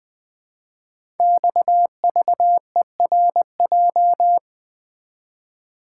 Morse code, heard as a keyed tone, about 20 wpm, 700 Hz. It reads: XVERJ
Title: Code Groups - 5 Letters - 20wpm